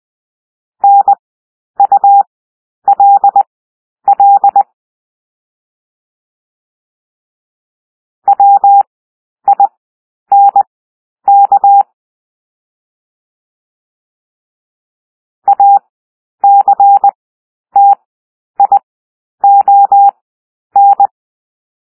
Morse code, heard as a keyed tone, about 20 wpm, spaced as for 10 wpm, 800 Hz. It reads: NULL WINK ACTION
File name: cwmp3 / W1 SOUND WORDS 20 wpm